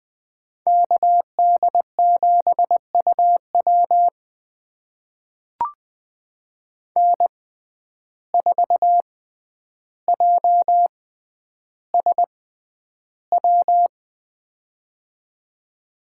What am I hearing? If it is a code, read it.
Morse code, heard as a keyed tone, about 20 wpm, spaced as for 10 wpm, 700 Hz. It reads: KD7UW N4JSW
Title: USA Call Signs - 5 Characters - 6x Character Spacing - 20wpm